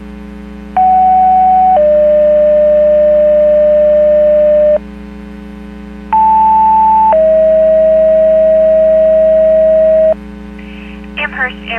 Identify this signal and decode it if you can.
Two-tone page, heard as a one-second tone, then a three-second tone, about 750 then 600 Hz, about 880 then 640 Hz, with speech. It is heard from 0.8 to 10.1 s.